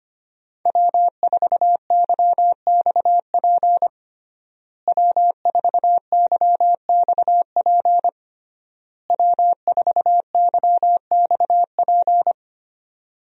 Morse code, heard as a keyed tone, about 25 wpm, 700 Hz. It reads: W4YXP W4YXP W4YXP
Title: USA Call Signs - 5 Characters - Repeated 3x - 25wpm